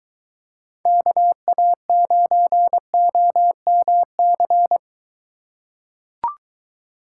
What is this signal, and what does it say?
Morse code, heard as a keyed tone, about 23 wpm, 700 Hz. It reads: KA9OMC